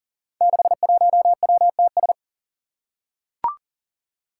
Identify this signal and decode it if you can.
Morse code, heard as a keyed tone, about 39 wpm, 700 Hz. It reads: 61WTS
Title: Five Challenging Letter-Number Combinations 40wpm